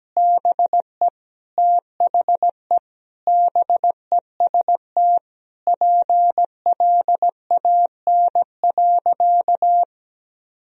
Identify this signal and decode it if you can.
Morse code, heard as a keyed tone, about 17 wpm, 700 Hz. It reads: BE THE BEST PLAN.
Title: Sentences from Top 1000 Words - Review Part 2 - 17wpm